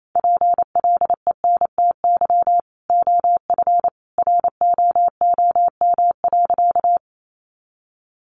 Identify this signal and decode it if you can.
Morse code, heard as a keyed tone, about 28 wpm, 700 Hz. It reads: PLENTY OF ROOM.
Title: Sentences from Top 1300 Words 28wpm